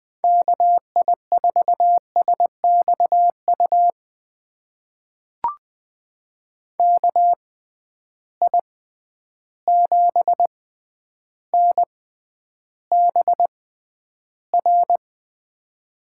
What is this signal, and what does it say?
Morse code, heard as a keyed tone, about 20 wpm, spaced as for 10 wpm, 700 Hz. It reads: KI4SXU KI7NBR
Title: USA Call Signs - 6 Characters - 6x Character Spacing - 20wpm